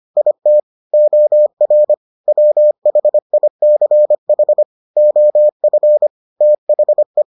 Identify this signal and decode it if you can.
Morse code, heard as a keyed tone, about 25 wpm, 600 Hz. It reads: IT OR WHICH OF THE